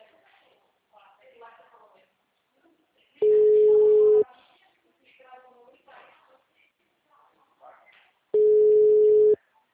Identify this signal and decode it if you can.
Ringback tone (Europe, 425 Hz), with speech.
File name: out- -2006- - - .57368